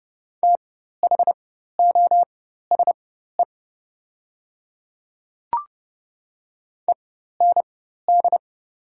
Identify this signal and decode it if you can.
Morse code, heard as a keyed tone, about 30 wpm, spaced as for 14 wpm, 700 Hz. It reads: THOSE END
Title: 4x Character Spacing - Top 300 Words - 30wpm